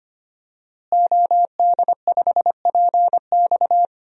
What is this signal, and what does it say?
Morse code, heard as a keyed tone, about 25 wpm, 700 Hz. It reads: OD5PX
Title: DXCC-100 Call Signs - 5 Characters - Repeated 3x - 25wpm